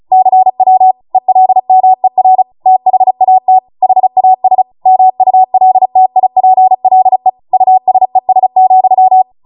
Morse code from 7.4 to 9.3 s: USES,